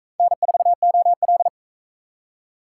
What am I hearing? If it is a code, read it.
Morse code, heard as a keyed tone, about 40 wpm, 700 Hz. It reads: N4OL